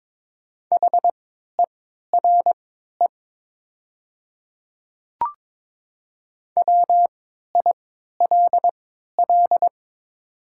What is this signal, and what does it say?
Morse code, heard as a keyed tone, about 22 wpm, spaced as for 12 wpm, 700 Hz. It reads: HERE WILL
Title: 3x Character Spacing - Top 500 Words - 22wpm